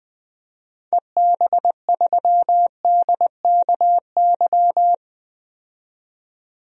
Morse code, heard as a keyed tone, about 20 wpm, 700 Hz. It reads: EB3DKY
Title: DXCC-100 Call Signs - 6 Characters - 20wpm